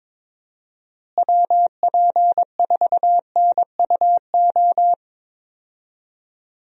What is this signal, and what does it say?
Morse code, heard as a keyed tone, about 22 wpm, 700 Hz. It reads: WP4NUO
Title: USA Call Signs - 6 Characters - Repeated 3x - 22wpm